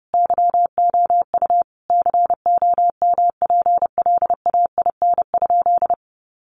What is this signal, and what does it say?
Morse code, heard as a keyed tone, about 28 wpm, 700 Hz. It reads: YOU COMPLAIN?
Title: Sentences from Top 1300 Words 30wpm